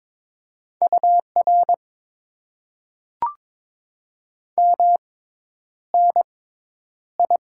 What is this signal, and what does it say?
Morse code, heard as a keyed tone, about 22 wpm, spaced as for 10 wpm, 700 Hz. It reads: UR MNI
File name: Top 90 QSO Elements - 6x Character Spacing - 22wpm